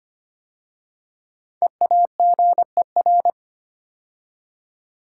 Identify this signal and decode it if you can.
Morse code, heard as a keyed tone, about 25 wpm, 700 Hz. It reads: EAGER
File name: 5 Letter Words - First 3 Letters Given - 25wpm